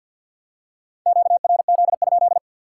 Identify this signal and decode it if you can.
Morse code, heard as a keyed tone, about 53 wpm, 700 Hz. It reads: QRZ?
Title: Top 90 QSO Elements 50wpm